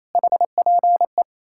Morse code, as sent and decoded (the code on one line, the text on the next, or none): .... .--. .
HPE